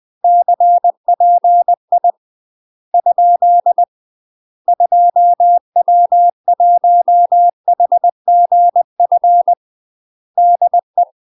Morse code, heard as a keyed tone, about 20 wpm, 700 Hz. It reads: CPI ? 2W1HGF DE
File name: UK MASTER QSOs TRAINING / 20WPM / UK MASTER QSOs 0064